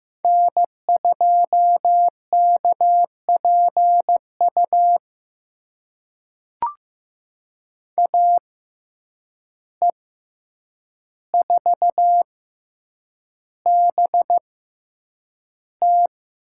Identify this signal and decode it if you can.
Morse code, heard as a keyed tone, about 15 wpm, spaced as for 8 wpm, 700 Hz. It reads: N2KPU AE4BT
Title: USA Call Signs - 5 Characters - 6x Character Spacing - 15wpm